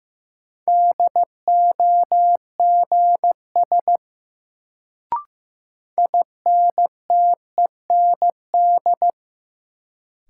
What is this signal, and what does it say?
Morse code, heard as a keyed tone, about 15 wpm, 700 Hz. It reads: DOGS INTEND
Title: Sentences from Top 1600 Words - New Word Drill Alt - 15wpm